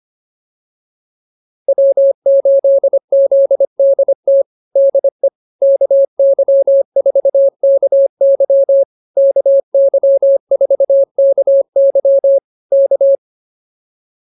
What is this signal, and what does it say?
Morse code, heard as a keyed tone, about 24 wpm, 550 Hz. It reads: W8ZDT DE KY4KY KY4KY K